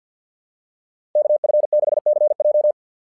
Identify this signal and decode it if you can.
Morse code, heard as a keyed tone, about 48 wpm, 600 Hz. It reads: KF6CJ
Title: SOTA QSO Practice 50wpm